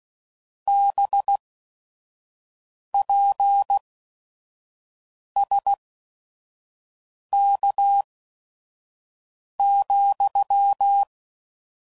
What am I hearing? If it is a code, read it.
Morse code, heard as a keyed tone, about 16 wpm, spaced as for 5 wpm, 800 Hz. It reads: BPSK,
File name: CW Code Groups @5WPM / QSOs / kc7sac